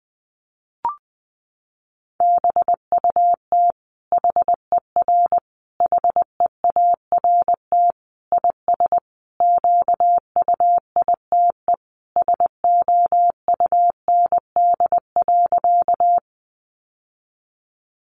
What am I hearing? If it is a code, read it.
Morse code, heard as a keyed tone, about 20 wpm, 700 Hz. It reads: BUT HER HEART IS QUITE SOUND.
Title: Sentences from Top 1200 Words 20wpm